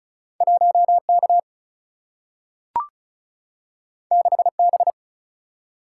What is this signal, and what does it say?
Morse code, heard as a keyed tone, about 36 wpm, 700 Hz. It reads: 1K 6B
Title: Single-Letter Single-Number 35wpm